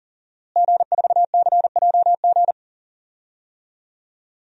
Morse code, heard as a keyed tone, about 40 wpm, 700 Hz. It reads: G4CJG
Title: DXCC-100 Call Signs - 5 Characters - Repeated 2x - 40wpm